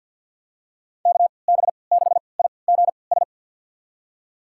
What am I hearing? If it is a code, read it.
Morse code, heard as a keyed tone, about 52 wpm, spaced as for 29 wpm, 700 Hz. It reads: KB6IGS
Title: USA Call Signs - 6 Characters - 3x Character Spacing - 50wpm